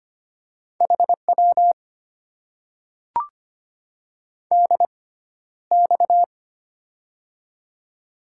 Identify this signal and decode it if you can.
Morse code, heard as a keyed tone, about 25 wpm, spaced as for 13 wpm, 700 Hz. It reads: HW DX
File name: Top 90 QSO Elements - 6x Character Spacing - 25wpm